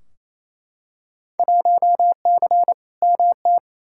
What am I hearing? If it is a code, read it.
Morse code, heard as a keyed tone, about 28 wpm, 700 Hz. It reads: 1C MT
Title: Field Day Exchanges Encore - 28wpm